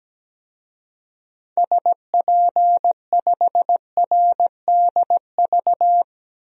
Morse code, heard as a keyed tone, about 17 wpm, 700 Hz. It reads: SP5RDV
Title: DXCC-100 Call Signs - 6 Characters - Repeated 3x - 17wpm